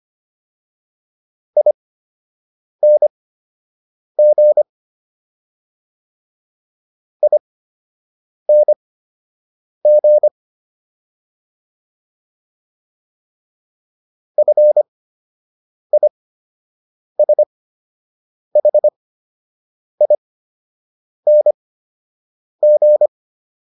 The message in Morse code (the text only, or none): ING ING FISHING